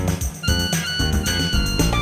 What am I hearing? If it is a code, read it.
There is no code.